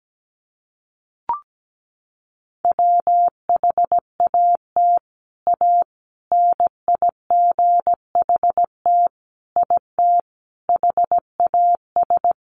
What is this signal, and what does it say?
Morse code, heard as a keyed tone, about 17 wpm, 700 Hz. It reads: WHAT A NIGHT IT HAS
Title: Sentences from Top 300 Words 17wpm